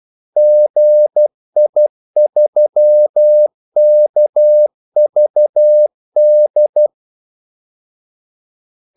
Morse code, heard as a keyed tone, about 12 wpm, 600 Hz. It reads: GI3KVD
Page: 2 ekstra ordmellemrum mellem kaldesignalerne